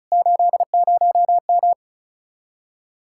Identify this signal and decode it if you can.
Morse code, heard as a keyed tone, about 40 wpm, 700 Hz. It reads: 80M